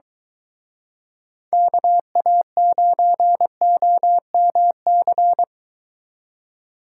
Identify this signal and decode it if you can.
Morse code, heard as a keyed tone, about 23 wpm, 700 Hz. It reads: KA9OMC